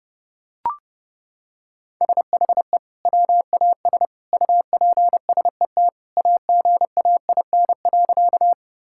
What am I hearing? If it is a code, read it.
Morse code, heard as a keyed tone, about 30 wpm, 700 Hz. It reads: SHE WAS UPSET AGAIN.
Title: Sentences from Top 1000 Words - Review Part 5 - 30wpm